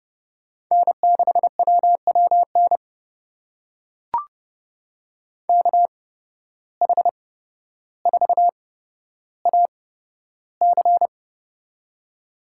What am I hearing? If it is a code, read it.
Morse code, heard as a keyed tone, about 29 wpm, spaced as for 13 wpm, 700 Hz. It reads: N6WWN KH4AC
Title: USA Call Signs - 5 Characters - 8x Character Spacing - 30wpm